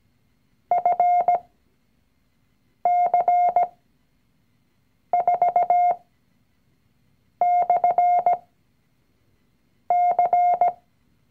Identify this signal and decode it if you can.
Morse code, heard as a keyed tone, about 17 wpm, spaced as for 5 wpm, 710 Hz. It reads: FC4/C
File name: CW-Online-Kurs / Anfänger / Lektion 019